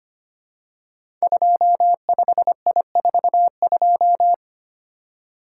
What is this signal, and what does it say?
Morse code, heard as a keyed tone, about 25 wpm, 700 Hz. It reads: 25I42
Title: Code Groups - 5 Letter:Numbers - 25wpm